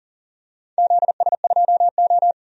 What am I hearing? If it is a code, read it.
Morse code, heard as a keyed tone, about 40 wpm, 700 Hz. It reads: ZS2O
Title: DXCC-100 Call Signs - 4 Characters - 40wpm